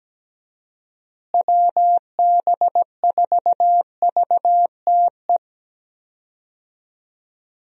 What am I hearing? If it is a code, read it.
Morse code, heard as a keyed tone, about 17 wpm, 700 Hz. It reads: WB4VTE
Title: USA Call Signs - 6 Characters - Repeated 3x - 17wpm